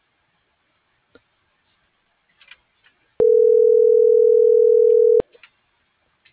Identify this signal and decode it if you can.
Ringback tone (North America), 440 + 480 Hz.